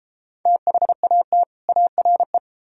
Morse code, heard as a keyed tone, about 32 wpm, 700 Hz. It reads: THAT ARE